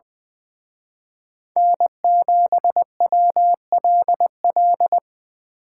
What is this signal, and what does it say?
Morse code, heard as a keyed tone, about 23 wpm, 700 Hz. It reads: N7WLL
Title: Speed Racing - USA 1x3 Call Signs 20wpm